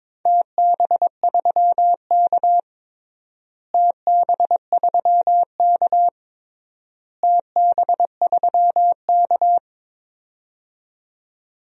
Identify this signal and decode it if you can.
Morse code, heard as a keyed tone, about 22 wpm, 700 Hz. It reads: TB3K TB3K TB3K
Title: DXCC-100 Call Signs - 4 Characters - Repeated 3x - 22wpm